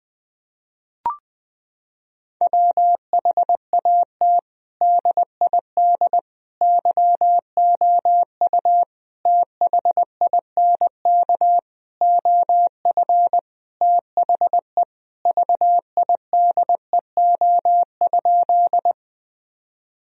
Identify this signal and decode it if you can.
Morse code, heard as a keyed tone, about 20 wpm, 700 Hz. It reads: WHAT DID YOU THINK OF THE VIDEO?
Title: Sentences from Top 1000 Words - Review Part 4 - 20wpm